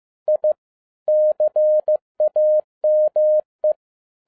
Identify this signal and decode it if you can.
Morse code, heard as a keyed tone, about 15 wpm, 610 Hz. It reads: I CAME